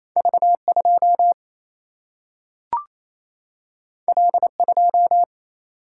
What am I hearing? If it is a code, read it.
Morse code, heard as a keyed tone, about 28 wpm, 700 Hz. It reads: V2 L2
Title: Single-Letter Single-Number 28wpm